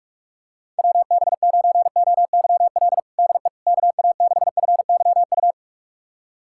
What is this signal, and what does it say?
Morse code, heard as a keyed tone, about 46 wpm, 700 Hz. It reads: WB9OYL DE KA6FYU
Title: USA Callsign DE Callsign - 6 Characters - 45wpm